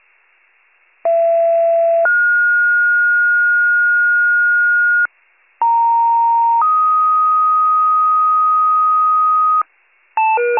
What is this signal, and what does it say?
Two-tone page, heard as a one-second tone, then a three-second tone, about 670 then 1430 Hz, about 930 then 1250 Hz, with noise. It is heard from 1.0 to 9.6 s.